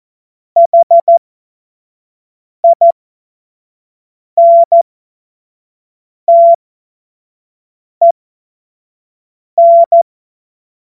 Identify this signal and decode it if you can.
Morse code, heard as a keyed tone, about 14 wpm, spaced as for 5 wpm, 680 Hz. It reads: HINTEN
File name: Morsekurs / Witze / Witz 30. August 2022 25 F70